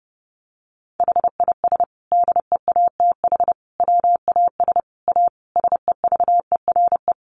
Morse code, heard as a keyed tone, about 30 wpm, 700 Hz. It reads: HIS DEATH WAS A SEVERE
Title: Sentences from Top 1400 Words 30wpm